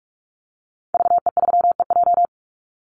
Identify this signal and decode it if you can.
Morse code, heard as a keyed tone, about 44 wpm, 700 Hz. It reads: VE3EJ
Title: Call Signs Repeated 2x - 45wpm